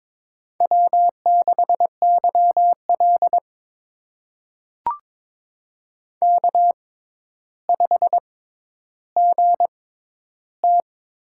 Morse code, heard as a keyed tone, about 22 wpm, spaced as for 11 wpm, 700 Hz. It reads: W6YL K5GT
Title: USA Call Signs - 4 Characters - 6x Character Spacing - 22wpm